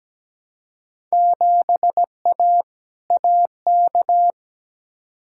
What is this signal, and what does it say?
Morse code, heard as a keyed tone, about 17 wpm, 700 Hz. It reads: 7A AK
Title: Field Day Exchanges Encore - 17wpm